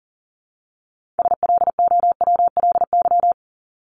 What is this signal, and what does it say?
Morse code, heard as a keyed tone, about 40 wpm, 700 Hz. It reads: SLOWLY
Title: Top 1001-1250 Words 40wpm